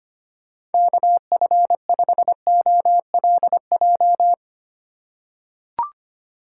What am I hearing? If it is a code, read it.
Morse code, heard as a keyed tone, about 25 wpm, 700 Hz. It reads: KF5OLJ
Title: USA Call Signs - 6 Characters - 3x Character Spacing - 25wpm